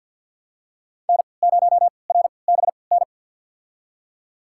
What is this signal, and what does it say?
Morse code, heard as a keyed tone, about 49 wpm, spaced as for 28 wpm, 700 Hz. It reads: N0RBN
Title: USA Call Signs - 5 Characters - 3x Character Spacing - 50wpm